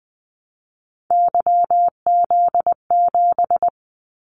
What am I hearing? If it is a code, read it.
Morse code, heard as a keyed tone, about 20 wpm, 700 Hz. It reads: YZ7
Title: Callsign Prefixes for Top 25 DXCC - 20wpm